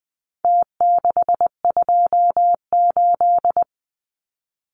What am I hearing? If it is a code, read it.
Morse code, heard as a keyed tone, about 20 wpm, 700 Hz. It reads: T628
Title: Numbers - Contest Style - 4-Digit 20wpm